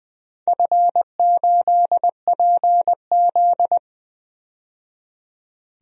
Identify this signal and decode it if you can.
Morse code, heard as a keyed tone, about 21 wpm, 700 Hz. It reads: F8PZ
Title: DXCC-100 Call Signs - 4 Characters - 20wpm